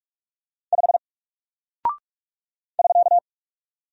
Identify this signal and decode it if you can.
Morse code, heard as a keyed tone, about 44 wpm, 700 Hz. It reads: 5 <SK>
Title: Single Character (Letter:Number:Punctuation:Prosign) Alt 45wpm